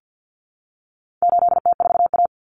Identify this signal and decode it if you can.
Morse code, heard as a keyed tone, about 50 wpm, 700 Hz. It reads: 8T4A
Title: Common Cut Numbers - 4-Digit 50wpm